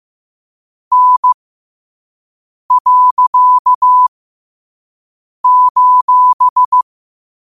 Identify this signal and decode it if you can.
Morse code, heard as a keyed tone, about 15 wpm, spaced as for 5 wpm, 1000 Hz. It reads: N.:
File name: qotdfiles / 5wpm